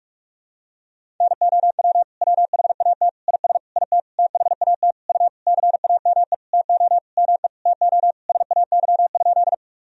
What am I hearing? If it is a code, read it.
Morse code, heard as a keyed tone, about 46 wpm, 700 Hz. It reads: NOW WHAT IS IT THAT U CAME TO ME TO SAY?
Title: Sentences from Top 200 Words and Abbreviations 45wpm